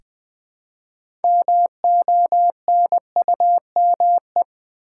Morse code, heard as a keyed tone, about 20 wpm, 700 Hz. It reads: MONUME
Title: Build A Word - 3 Letters At A Time - 8-Letter Words - 20wpm